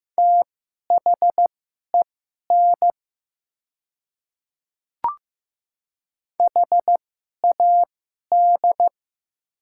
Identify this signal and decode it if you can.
Morse code, heard as a keyed tone, about 15 wpm, spaced as for 11 wpm, 700 Hz. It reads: THEN HAD
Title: 2x Character Spacing - Top 100 Words - 15wpm